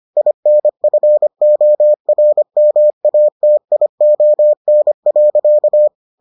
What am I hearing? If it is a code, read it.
Morse code, heard as a keyed tone, about 25 wpm, 600 Hz. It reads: INFORMATION.